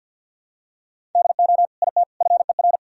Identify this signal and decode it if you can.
Morse code, heard as a keyed tone, about 46 wpm, 700 Hz. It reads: DO IT FER
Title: Sentences from Top 200 Words and Abbreviations 50wpm